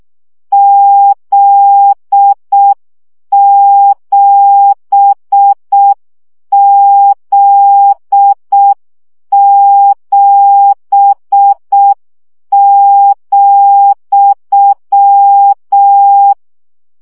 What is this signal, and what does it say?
Morse code, heard as a keyed tone, about 6 wpm, 800 Hz. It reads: Z7Z7,
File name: CW gyakorlas z 7 vesszo 5 baud